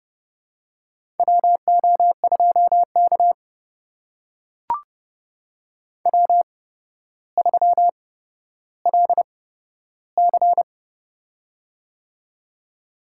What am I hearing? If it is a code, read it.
Morse code, heard as a keyed tone, about 30 wpm, spaced as for 13 wpm, 700 Hz. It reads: WO2K W3LC
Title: USA Call Signs - 4 Characters - 8x Character Spacing - 30wpm